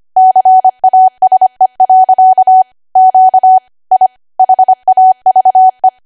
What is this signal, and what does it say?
Morse code, heard as a keyed tone, about 25 wpm, 750 Hz. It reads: CASE. Q I HAVE
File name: Archive / 25 WPM / 25WPM